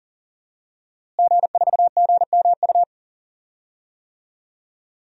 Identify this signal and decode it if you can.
Morse code, heard as a keyed tone, about 38 wpm, 700 Hz. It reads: G4GMU
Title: DXCC-100 Call Signs - 5 Characters - Repeated 3x - 40wpm